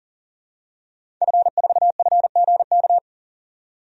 Morse code, heard as a keyed tone, about 40 wpm, 700 Hz. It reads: F4FGK
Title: DXCC-100 Call Signs - 5 Characters - 40wpm